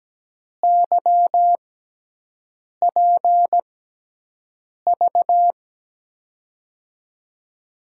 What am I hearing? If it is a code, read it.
Morse code, heard as a keyed tone, about 17 wpm, spaced as for 6 wpm, 700 Hz. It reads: YPV
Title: USA Call Signs - 6 Characters - 6x Character Spacing - 17wpm